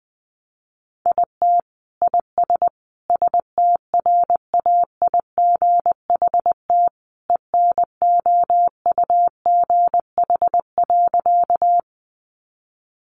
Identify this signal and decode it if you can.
Morse code, heard as a keyed tone, about 19 wpm, 700 Hz. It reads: IT IS STRAIGHT ENOUGH.
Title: Sentences from Top 1000 Words 20wpm